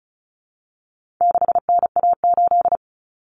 Morse code, heard as a keyed tone, about 34 wpm, 700 Hz. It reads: 6NA8
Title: Common Cut Numbers - 4-Digit 35wpm